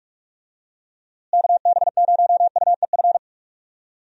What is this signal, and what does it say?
Morse code, heard as a keyed tone, about 46 wpm, 700 Hz. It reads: KB0UEF
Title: USA Call Signs - 6 Characters - Repeated 2x - 45wpm